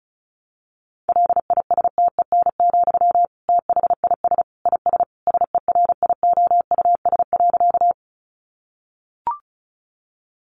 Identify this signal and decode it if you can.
Morse code, heard as a keyed tone, about 34 wpm, 700 Hz. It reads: LISTEN, THIS IS SERIOUS.
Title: Sentences from Top 1100 Words 35wpm